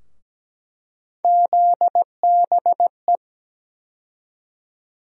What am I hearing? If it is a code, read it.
Morse code, heard as a keyed tone, about 17 wpm, 700 Hz. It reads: ZBE
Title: USA Call Sign Postfixes - 3 Letters 17wpm